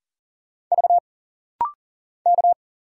Morse code, heard as a keyed tone, about 40 wpm, 700 Hz. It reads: V K
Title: Single Letter-Number Alt - Rapid-Fire 40wpm